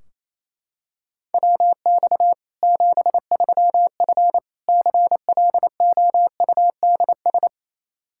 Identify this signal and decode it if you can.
Morse code, heard as a keyed tone, about 27 wpm, 700 Hz. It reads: WX 73F CLOUDS
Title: Practice QSO - Exchange 2 - Info - 28wpm